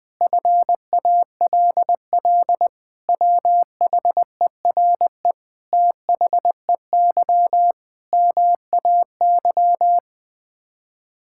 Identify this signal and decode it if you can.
Morse code, heard as a keyed tone, about 20 wpm, 700 Hz. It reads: FALL WHERE THEY MAY
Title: Sets of 8 Words - 20wpm